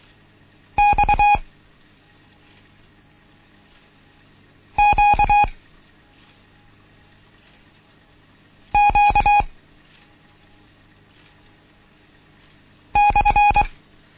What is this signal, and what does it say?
Morse code, heard as a keyed tone, about 23 wpm, 830 Hz. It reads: X Q Q /
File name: Morsen / Kurs A 16